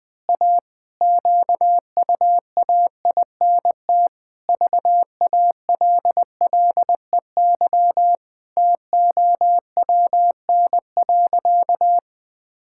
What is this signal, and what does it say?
Morse code, heard as a keyed tone, about 20 wpm, 700 Hz. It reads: A QUAINT VALLEY TOWN.
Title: Pangram Challenge v2 - 20wpm